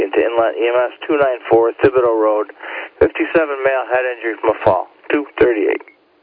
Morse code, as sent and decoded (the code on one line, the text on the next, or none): none